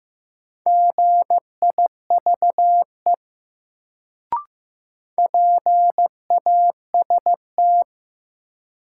Morse code, heard as a keyed tone, about 15 wpm, 700 Hz. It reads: GIVE PAST
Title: Sets of 2 Words - New Word Drill - 15wpm